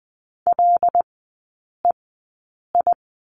Morse code, heard as a keyed tone, about 20 wpm, spaced as for 8 wpm, 700 Hz. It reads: LEI
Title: 081 Lesson 14e - Number 2 - Three Characters 20wpm